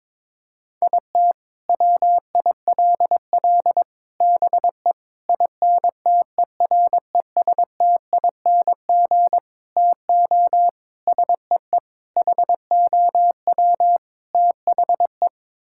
Morse code, heard as a keyed tone, about 22 wpm, 700 Hz. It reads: IT WILL BE INTERESTING TO SEE HOW THE
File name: practice / 22wpm / Sets of 8 Words - 22wpm